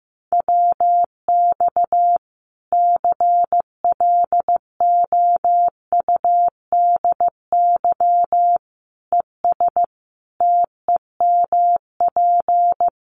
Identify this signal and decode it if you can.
Morse code, heard as a keyed tone, about 15 wpm, 700 Hz. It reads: WX CLOUDY ES TEMP
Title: Common QSO Phrases 15wpm